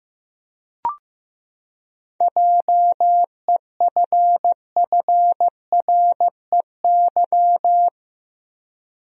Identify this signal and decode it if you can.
Morse code, heard as a keyed tone, about 15 wpm, 700 Hz. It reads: JEFFREY
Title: USA Common Names - 15wpm